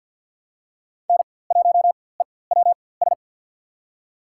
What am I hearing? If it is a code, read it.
Morse code, heard as a keyed tone, about 46 wpm, spaced as for 23 wpm, 700 Hz. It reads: N1EWS
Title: USA Call Signs - 5 Characters - 4x Character Spacing - 50wpm